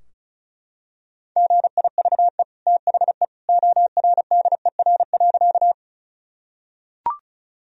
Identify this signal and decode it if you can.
Morse code, heard as a keyed tone, about 36 wpm, 700 Hz. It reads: GIVE THE ORDER.
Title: Sentences from Top 1000 Words - Review Part 2 - 35wpm